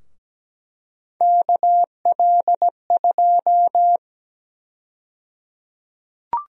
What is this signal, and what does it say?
Morse code, heard as a keyed tone, about 17 wpm, 700 Hz. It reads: KL2
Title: USA Call Sign Prefixes - Repeated 2x - 17wpm